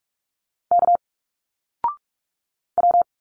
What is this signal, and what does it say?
Morse code, heard as a keyed tone, about 46 wpm, 700 Hz. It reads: K W
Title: Single Letters 45wpm